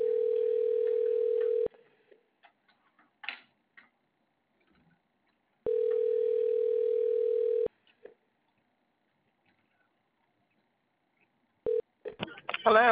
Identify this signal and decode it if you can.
Ringback tone (North America), 440 + 480 Hz.